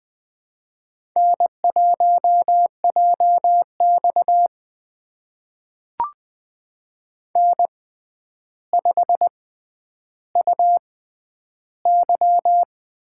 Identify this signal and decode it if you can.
Morse code, heard as a keyed tone, about 20 wpm, spaced as for 10 wpm, 700 Hz. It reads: N1JX N5UY
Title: USA Call Signs - 4 Characters - 6x Character Spacing - 20wpm